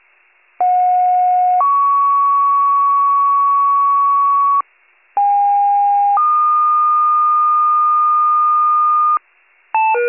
Two-tone page, heard as a one-second tone, then a three-second tone, about 730 then 1120 Hz, about 810 then 1220 Hz. It is heard from 0.6 to 9.2 s.